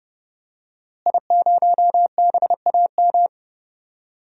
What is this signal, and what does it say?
Morse code, heard as a keyed tone, about 29 wpm, 700 Hz. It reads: I0BAM
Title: DXCC-100 Call Signs - 5 Characters - Repeated 3x - 30wpm